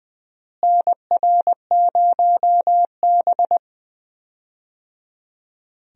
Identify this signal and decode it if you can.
Morse code, heard as a keyed tone, about 20 wpm, 700 Hz. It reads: NR0B
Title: USA Call Signs - 4 Characters - Repeated 3x - 20wpm